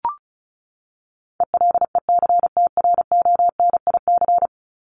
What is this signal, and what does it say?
Morse code, heard as a keyed tone, about 34 wpm, 700 Hz. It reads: ELECTRONIC
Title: Top 1001-1250 Words 35wpm